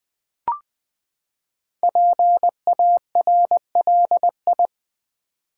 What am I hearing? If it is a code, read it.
Morse code, heard as a keyed tone, about 20 wpm, 700 Hz. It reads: PARLI
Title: Build A Word - 5 Letters At A Time - 10-Letter Words - 20wpm